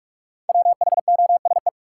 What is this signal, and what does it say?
Morse code, heard as a keyed tone, about 47 wpm, 700 Hz. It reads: WHOSE
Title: Five Letter Words - 45wpm